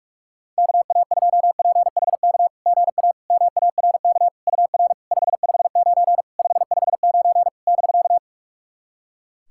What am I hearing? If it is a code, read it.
Morse code, heard as a keyed tone, about 46 wpm, 690 Hz. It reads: KA2PHK GA MARK UR 559 559 <BK>
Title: SOTA QSO Practice 45wpm